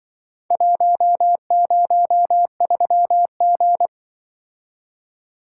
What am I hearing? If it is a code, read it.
Morse code, heard as a keyed tone, about 24 wpm, 700 Hz. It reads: 103G
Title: Speed-Racing - Four Letter-Number Combinations - 20wpm